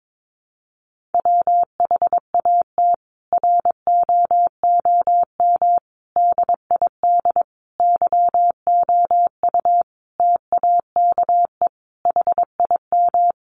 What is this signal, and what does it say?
Morse code, heard as a keyed tone, about 22 wpm, 700 Hz. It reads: WHAT ROOM DID YOU TAKE HIM
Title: Sentences from Top 600 Words 22wpm